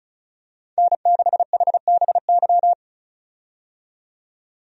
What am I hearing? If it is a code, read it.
Morse code, heard as a keyed tone, about 36 wpm, 700 Hz. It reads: N6HBY
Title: USA Call Signs - 5 Characters - 35wpm